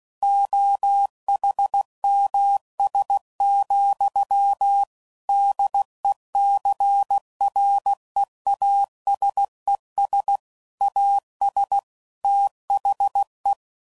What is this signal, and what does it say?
Morse code, heard as a keyed tone, about 16 wpm, 800 Hz. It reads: OHMS, DECREASES AS THE
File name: CW Code Groups @15WPM / Technical Subjects / XC